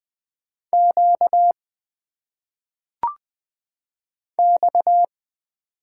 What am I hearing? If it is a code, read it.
Morse code, heard as a keyed tone, about 20 wpm, 700 Hz. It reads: Q X
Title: Single Letters - Phonetic Pronunciation 20wpm